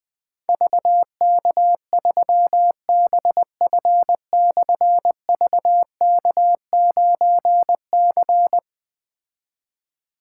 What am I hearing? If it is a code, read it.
Morse code, heard as a keyed tone, about 20 wpm, 700 Hz. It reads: VK3BF/VK9C
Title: Difficult Call Signs 20wpm